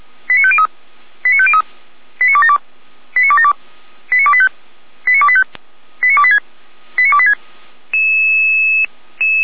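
Five-tone selective call (ZVEI), heard as quick runs of five tones, tones of about 70 ms. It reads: 89503 89503 89272 89272 89285 89285 89286 89286
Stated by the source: SDR-received ZVEI five-tone call 89262, fire brigade Orlamünde.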